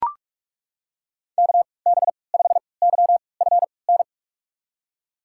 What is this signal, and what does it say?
Morse code, heard as a keyed tone, about 42 wpm, spaced as for 25 wpm, 700 Hz. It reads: KB5YFN
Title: USA Call Signs - 6 Characters - 3x Character Spacing - 45wpm